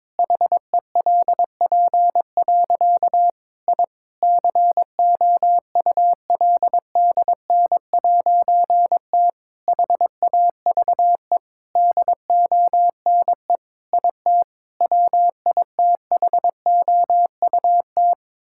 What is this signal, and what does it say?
Morse code, heard as a keyed tone, about 22 wpm, 700 Hz. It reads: HELP. I COULDN'T HAVE DONE IT WITHOUT
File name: Difficult Sentences with Hints - 22wpm